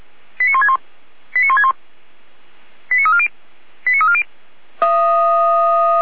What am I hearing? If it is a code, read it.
Five-tone selective call (ZVEI), heard as quick runs of five tones, tones of about 80 ms. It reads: 89161 89161 89340 89340
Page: ZVEI